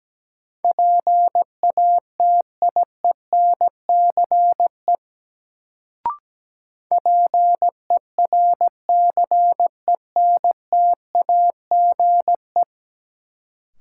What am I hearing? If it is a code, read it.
Morse code, heard as a keyed tone, about 17 wpm, 700 Hz. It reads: PATIENCE PERCENTAGE
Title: Sentences from Top 1300 Words - New Word Drill Alt - 17wpm